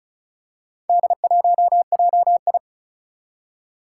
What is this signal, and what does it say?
Morse code, heard as a keyed tone, about 36 wpm, 690 Hz. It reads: D1JI
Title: Four Challenging Letter-Number Combinations 35wpm